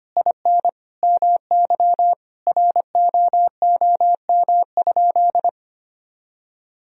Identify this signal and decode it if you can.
Morse code, heard as a keyed tone, about 25 wpm, 700 Hz. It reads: IN MY ROOM?
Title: Sentences from Top 1000 Words - Review Part 1 - 25wpm